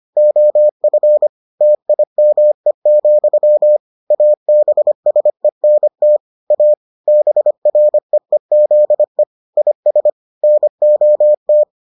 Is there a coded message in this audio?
Yes: Morse code from 0.2 to 11.6 s, heard as a keyed tone, about 25 wpm, 600 Hz. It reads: OF TIME, ABSENT A BREEZE IS NOT